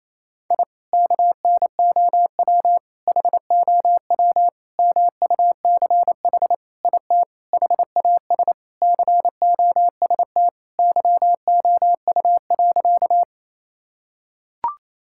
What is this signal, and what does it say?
Morse code, heard as a keyed tone, about 28 wpm, 700 Hz. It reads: I KNOW HOW MUCH IT HAS COST YOU.
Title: Sentences from Top 1000 Words - Review Part 5 - 28wpm